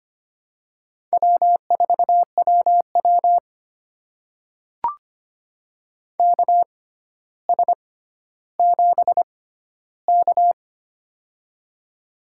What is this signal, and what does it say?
Morse code, heard as a keyed tone, about 25 wpm, spaced as for 13 wpm, 700 Hz. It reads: W4WW KS7K
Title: USA Call Signs - 4 Characters - 6x Character Spacing - 25wpm